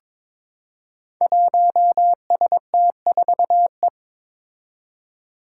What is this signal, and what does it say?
Morse code, heard as a keyed tone, about 22 wpm, 700 Hz. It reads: 1ST4E
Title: Five Challenging Letter-Number Combinations 22wpm